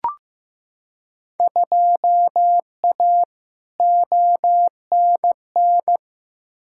Morse code, heard as a keyed tone, about 15 wpm, 700 Hz. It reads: 2A ONN
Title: Field Day Exchanges Encore - 15wpm